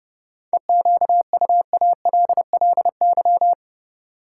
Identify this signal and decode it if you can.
Morse code, heard as a keyed tone, about 32 wpm, 700 Hz. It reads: EQUALLY
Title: Sentences from Top 1400 Words - New Word Drill Alt - 30wpm